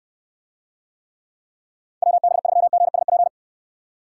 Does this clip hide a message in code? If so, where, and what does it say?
Yes: Morse code from 2.0 to 3.3 s, heard as a keyed tone, about 69 wpm, 700 Hz. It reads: WB3GSP